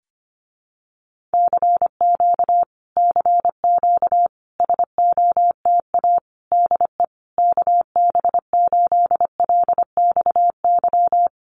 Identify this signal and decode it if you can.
Morse code, heard as a keyed tone, about 25 wpm, 700 Hz. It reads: CQ CQ SOTA DE KB8LXY
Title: Common QSO Phrases Encore 25wpm